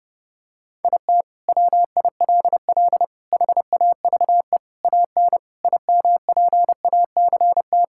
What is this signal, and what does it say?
Morse code, heard as a keyed tone, about 28 wpm, 700 Hz. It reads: IT WILL HAVE AN IMPACT
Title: Sentences from Top 1000 Words - Review Part 6 - 30wpm